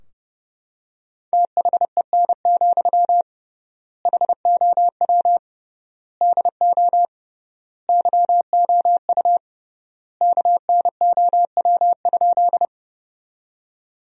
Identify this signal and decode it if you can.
Morse code, heard as a keyed tone, about 32 wpm, 700 Hz. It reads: THEN, HOW DO YOU KNOW?
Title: 3x Word-Spacing - Sentences from Top 100 Words - 30wpm